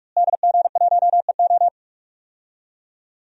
Morse code, heard as a keyed tone, about 46 wpm, 700 Hz. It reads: DG1EO
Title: DXCC-100 Call Signs - 5 Characters - Repeated 3x - 45wpm